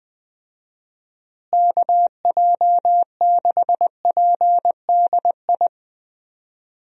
Morse code, heard as a keyed tone, about 20 wpm, 700 Hz. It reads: KJ6PDI
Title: USA Call Signs - 6 Characters - Repeated 3x - 20wpm